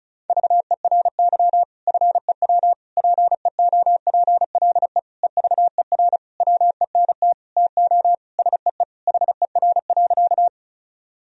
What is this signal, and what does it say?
Morse code, heard as a keyed tone, about 36 wpm, 700 Hz. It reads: VERY FEW PEOPLE EVER WENT TO SEE HER.
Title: Sentences from Top 1000 Words - Review Part 1 - 35wpm